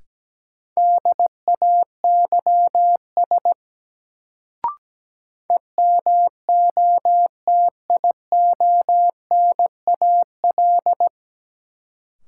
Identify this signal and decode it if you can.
Morse code, heard as a keyed tone, about 17 wpm, 700 Hz. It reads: DAYS EMOTIONAL
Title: Sentences from Top 1000 Words - Review Part 5 - New Word Drill - 17wpm